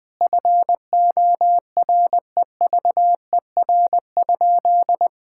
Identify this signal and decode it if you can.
Morse code, heard as a keyed tone, about 21 wpm, 700 Hz. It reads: FOREVER?
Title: Sentences from Top 1600 Words 20wpm